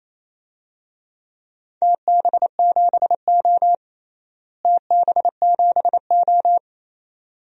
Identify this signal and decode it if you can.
Morse code, heard as a keyed tone, about 28 wpm, 700 Hz. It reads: TB7O TB7O
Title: DXCC-100 Call Signs - 4 Characters - Repeated 2x - 28wpm